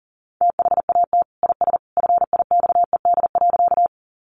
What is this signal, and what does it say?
Morse code, heard as a keyed tone, about 40 wpm, 700 Hz. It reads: THAT IS FIXED.
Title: Sentences from Top 1100 Words 40wpm